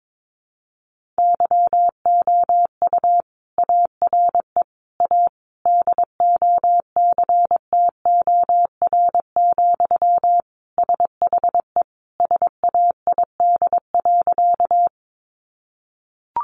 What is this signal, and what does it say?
Morse code, heard as a keyed tone, about 22 wpm, 700 Hz. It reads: YOU ARE A DOCTOR, SHE SAID.
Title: Sentences from Top 1500 Words 22wpm